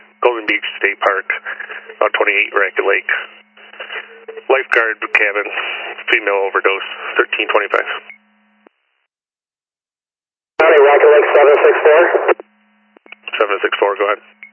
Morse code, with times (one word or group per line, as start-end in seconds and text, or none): none